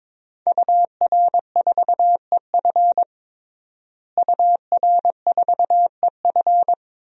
Morse code, heard as a keyed tone, about 22 wpm, 700 Hz. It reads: UR4EF UR4EF
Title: DXCC-100 Call Signs - 5 Characters - Repeated 2x - 22wpm